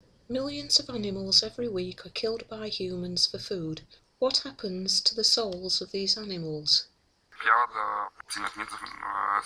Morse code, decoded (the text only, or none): none